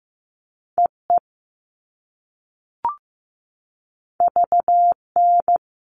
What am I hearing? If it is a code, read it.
Morse code, heard as a keyed tone, about 15 wpm, 700 Hz. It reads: EE VN
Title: All Cut Numbers - 2-Digit 15wpm